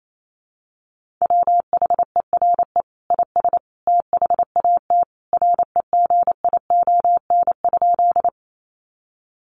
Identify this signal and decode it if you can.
Morse code, heard as a keyed tone, about 28 wpm, 700 Hz. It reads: WHERE IS THAT REGION?
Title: Sentences from Top 1500 Words 28wpm